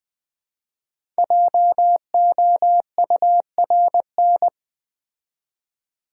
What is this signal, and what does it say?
Morse code, heard as a keyed tone, about 20 wpm, 700 Hz. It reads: JOURN
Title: Build A Word - 5 Letters At A Time - 10-Letter Words - 20wpm